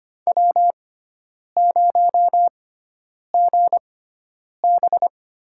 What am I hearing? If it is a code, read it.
Morse code, heard as a keyed tone, about 25 wpm, spaced as for 9 wpm, 700 Hz. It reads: W0GB
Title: USA Call Signs - 6x Character Spacing - 25wpm